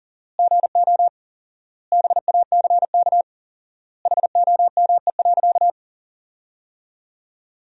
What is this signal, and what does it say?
Morse code, heard as a keyed tone, about 40 wpm, 700 Hz. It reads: GO BACK HOME.
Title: 4x Word-Spacing - Sentences from Top 500 Words - 40wpm